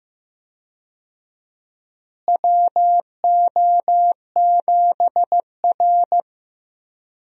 Morse code, heard as a keyed tone, about 15 wpm, 700 Hz. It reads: WO7R
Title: USA Call Signs - 4 Characters - Repeated 3x - 15wpm